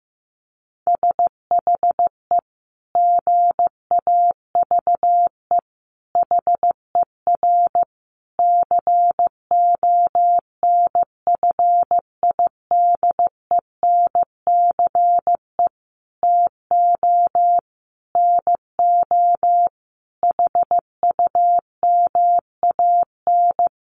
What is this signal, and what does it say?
Morse code, heard as a keyed tone, about 15 wpm, 700 Hz. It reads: SHE GAVE HER CONFIDENCE TO NO HUMAN
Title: Sentences from Top 1500 Words 15wpm